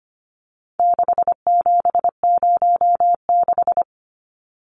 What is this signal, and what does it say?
Morse code, heard as a keyed tone, about 25 wpm, 700 Hz. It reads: 6706
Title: Numbers - 4-Digit 25wpm